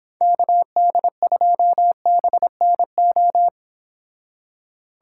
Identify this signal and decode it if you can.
Morse code, heard as a keyed tone, about 26 wpm, 700 Hz. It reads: KD2BNO